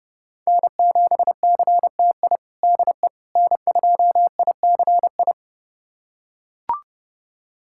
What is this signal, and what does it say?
Morse code, heard as a keyed tone, about 32 wpm, 700 Hz. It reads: N7CTI DE N2ICI
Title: USA Callsign DE Callsign - 5 Characters - 30wpm